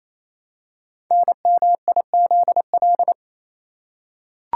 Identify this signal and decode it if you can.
Morse code, heard as a keyed tone, about 29 wpm, 700 Hz. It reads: NMIZL
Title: Code Groups - 5 Letters - 28wpm